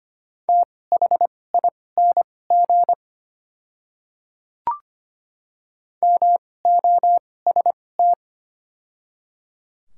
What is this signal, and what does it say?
Morse code, heard as a keyed tone, about 25 wpm, spaced as for 18 wpm, 700 Hz. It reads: THING MOST
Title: 2x Character Spacing - Top 100 Words - 25wpm